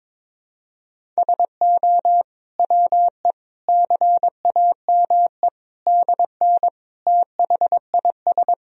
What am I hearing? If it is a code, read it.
Morse code, heard as a keyed tone, about 22 wpm, 700 Hz. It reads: SO WE CAME DN THIS
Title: Sentences from Top 400 Words and Abbreviations 22wpm